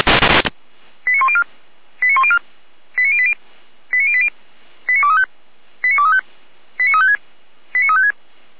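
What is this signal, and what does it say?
Five-tone selective call (ZVEI), heard as quick runs of five tones, tones of about 80 ms. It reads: 89104 89104 89080 89080 89235 89235 89356 89356
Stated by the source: ZVEI five-tone selective call code 89092, received by SDR.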